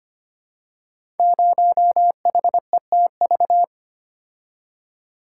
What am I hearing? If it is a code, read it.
Morse code, heard as a keyed tone, about 25 wpm, 700 Hz. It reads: 0HETV
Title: Code Groups - 5 Letter:Numbers - 25wpm